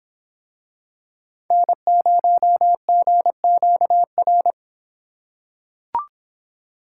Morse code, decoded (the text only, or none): N0GQR